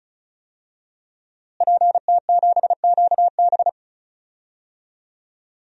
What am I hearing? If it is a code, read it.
Morse code, heard as a keyed tone, about 36 wpm, 700 Hz. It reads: PT7QB
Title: DXCC-100 Call Signs - 5 Characters - Repeated 3x - 35wpm